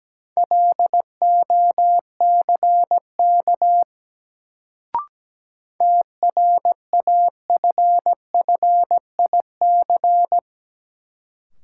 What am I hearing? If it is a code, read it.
Morse code, heard as a keyed tone, about 17 wpm, 700 Hz. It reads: LOCK TRAFFIC
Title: Sentences from Top 1300 Words - New Word Drill - 17wpm